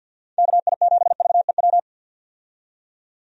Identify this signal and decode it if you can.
Morse code, heard as a keyed tone, about 50 wpm, 700 Hz. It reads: KI7VEW
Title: USA Call Signs - 6 Characters - 50wpm